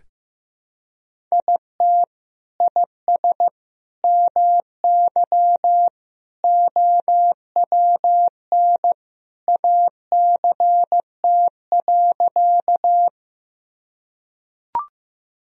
Morse code, heard as a keyed tone, about 15 wpm, 700 Hz. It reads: IT IS MY OWN ACT.
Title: Sentences from Top 1000 Words - Review Part 1 - 15wpm